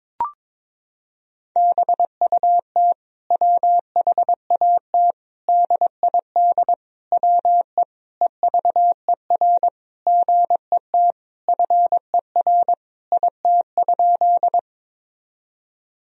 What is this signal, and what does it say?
Morse code, heard as a keyed tone, about 22 wpm, 700 Hz. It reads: BUT WHAT DID WE EVER GET FER IT?
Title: Sentences from Top 400 Words and Abbreviations 22wpm